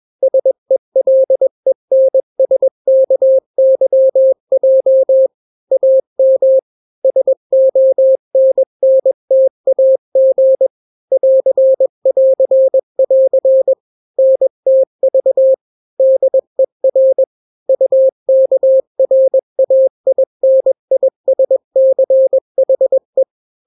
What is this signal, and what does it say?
Morse code, heard as a keyed tone, about 21 wpm, 530 Hz. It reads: SELENSKYJ AM SONNTAG <AR><AR><AR> NTV DER UKRAINISCHE
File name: daily / 20wpm